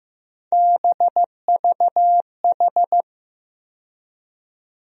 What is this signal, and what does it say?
Morse code, heard as a keyed tone, about 15 wpm, 700 Hz. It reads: BVH
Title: USA Call Sign Postfixes - 3 Letters 15wpm